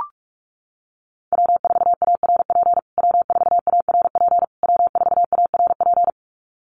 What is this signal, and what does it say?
Morse code, heard as a keyed tone, about 44 wpm, 700 Hz. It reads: W4ARP W4ARP W4ARP
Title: USA Call Signs Repeated 3x 45wpm